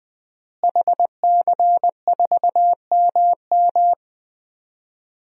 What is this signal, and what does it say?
Morse code, heard as a keyed tone, about 20 wpm, 700 Hz. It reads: HC4MM
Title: DXCC-100 Call Signs - 5 Characters - Repeated 2x - 20wpm